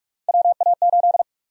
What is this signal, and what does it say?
Morse code, heard as a keyed tone, about 42 wpm, 700 Hz. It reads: WA8
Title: USA Call Sign Prefixes - 45wpm